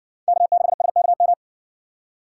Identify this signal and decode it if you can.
Morse code, heard as a keyed tone, about 58 wpm, 700 Hz. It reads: K6SKM